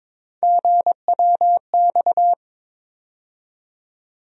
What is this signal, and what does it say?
Morse code, heard as a keyed tone, about 22 wpm, 700 Hz. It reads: GWX
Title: USA Call Sign Postfixes - 3 Letters 22wpm